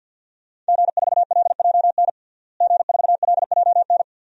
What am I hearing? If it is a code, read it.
Morse code, heard as a keyed tone, about 46 wpm, 700 Hz. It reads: G4LJN G4LJN
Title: DXCC-100 Call Signs - 5 Characters - Repeated 2x - 50wpm